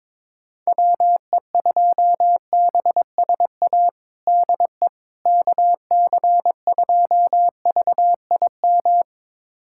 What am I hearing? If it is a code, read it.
Morse code, heard as a keyed tone, about 22 wpm, 700 Hz. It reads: WE2BSA DE KC2VIM
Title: USA Callsign DE Callsign - 6 Characters - 22wpm